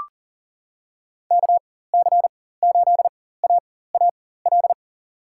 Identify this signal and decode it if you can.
Morse code, heard as a keyed tone, about 37 wpm, spaced as for 18 wpm, 700 Hz. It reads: KC8AAL
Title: USA Call Signs - 4x Character Spacing - 40wpm